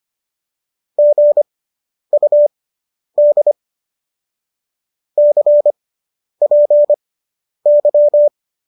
Morse code, heard as a keyed tone, about 25 wpm, spaced as for 10 wpm, 590 Hz. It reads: GUD CPY